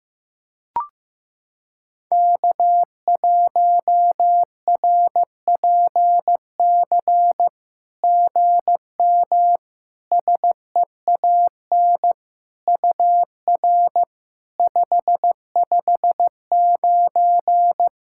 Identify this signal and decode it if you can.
Morse code, heard as a keyed tone, about 15 wpm, 700 Hz. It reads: K1RPC GM SEAN UR 559
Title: SOTA QSO Practice 15wpm